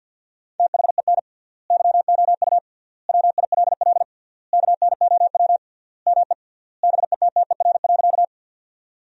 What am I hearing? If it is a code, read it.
Morse code, heard as a keyed tone, about 50 wpm, 700 Hz. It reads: THEN YOU WILL KNOW ME BETTER.
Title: 3x Word-Spacing - Sentences from Top 400 Words - 50wpm